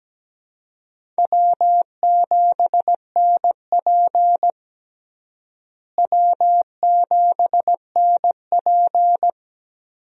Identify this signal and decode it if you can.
Morse code, heard as a keyed tone, about 17 wpm, 700 Hz. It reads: W7NP W7NP
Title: USA Call Signs - 4 Characters - Repeated 3x - 17wpm